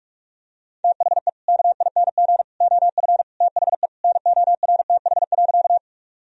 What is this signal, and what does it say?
Morse code, heard as a keyed tone, about 45 wpm, 700 Hz. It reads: THE KING OF THE NORTH.
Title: Sentences from Top 1000 Words - Review Part 1 - 45wpm